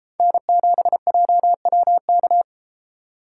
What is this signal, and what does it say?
Morse code, heard as a keyed tone, about 32 wpm, 700 Hz. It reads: N7JWK